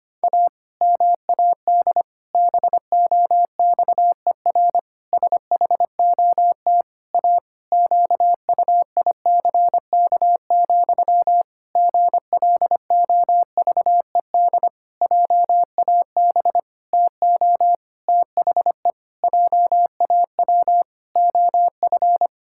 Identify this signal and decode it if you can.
Morse code, heard as a keyed tone, about 25 wpm, 700 Hz. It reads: A MAD BOXER SHOT A QUICK, GLOVED JAB TO THE JAW OF
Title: Pangram Challenge v2 - 25wpm